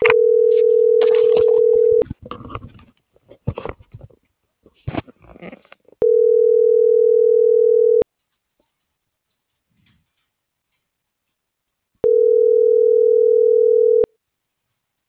Ringback tone (North America), 440 + 480 Hz.